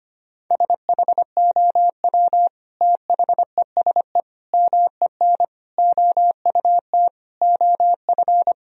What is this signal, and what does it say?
Morse code, heard as a keyed tone, about 25 wpm, 700 Hz. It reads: SHOW THESE MEN OUT OF
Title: Sentences from Top 1000 Words - Review Part 1 - 25wpm